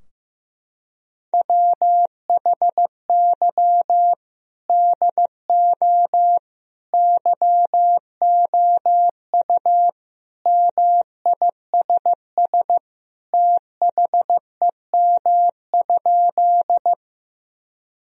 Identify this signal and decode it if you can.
Morse code, heard as a keyed tone, about 15 wpm, 700 Hz. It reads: WHY DO YOU MISS THEM?
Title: Sentences from Top 1000 Words - Review Part 3 - 15wpm